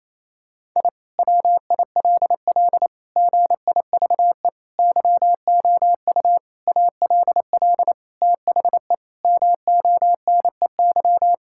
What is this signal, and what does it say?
Morse code, heard as a keyed tone, about 28 wpm, 700 Hz. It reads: I WILL GIVE YOU ALL THE MONEY
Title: Sentences from Top 1000 Words - Review Part 6 - 28wpm